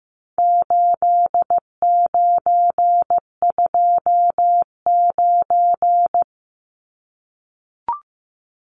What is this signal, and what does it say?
Morse code, heard as a keyed tone, about 15 wpm, 700 Hz. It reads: 8929
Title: Numbers - 4-Digit 15wpm